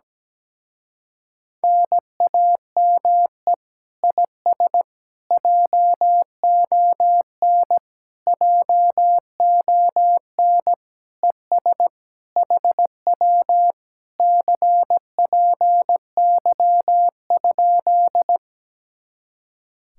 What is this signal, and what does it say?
Morse code, heard as a keyed tone, about 17 wpm, 700 Hz. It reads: NAME IS JON JON ES HW CPY?
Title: Speed-Racing Common QSO Phrases - 17wpm